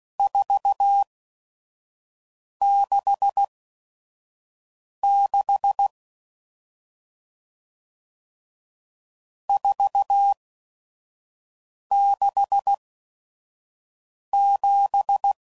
Morse code, heard as a keyed tone, about 16 wpm, spaced as for 5 wpm, 800 Hz. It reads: 466 467
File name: CW Code Groups @5WPM / RST's / Various RST No's